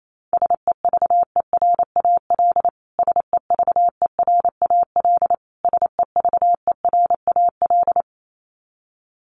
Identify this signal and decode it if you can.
Morse code, heard as a keyed tone, about 28 wpm, 700 Hz. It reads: SEVERAL SEVERAL SEVERAL
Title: Top 500 Words Repeated 28wpm